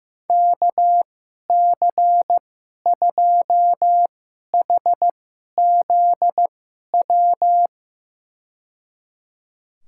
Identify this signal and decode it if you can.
Morse code, heard as a keyed tone, about 15 wpm, spaced as for 11 wpm, 700 Hz. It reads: KC2HZW
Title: USA Call Signs - 2x Character Spacing - 15wpm